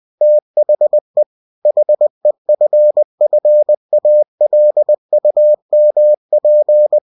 Morse code, heard as a keyed tone, about 20 wpm, 600 Hz. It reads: THE HEFFALUMP
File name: cw-books / winnie-the-pooh / winnie-20wpm-page-0035